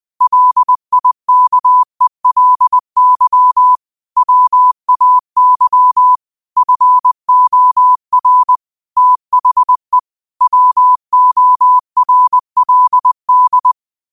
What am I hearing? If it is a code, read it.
Morse code, heard as a keyed tone, about 20 wpm, 1000 Hz. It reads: LIKELY WAY FOR THE WORLD